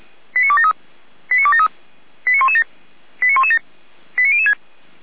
Five-tone selective call (ZVEI), heard as quick runs of five tones, tones of about 70 ms. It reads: 89273 89273 89117 89117 89006